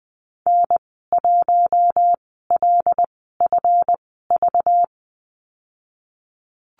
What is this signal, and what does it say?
Morse code, heard as a keyed tone, about 19 wpm, spaced as for 14 wpm, 700 Hz. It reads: N1LFV
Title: 193 Lesson 31g - Letter v - Call Signs 20wpm